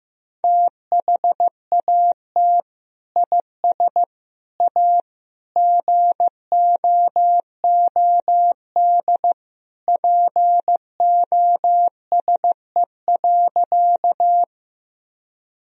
Morse code, heard as a keyed tone, about 15 wpm, 700 Hz. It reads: THAT IS A GOOD POSE.
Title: Sentences from Top 1000 Words - Review Part 2 - 15wpm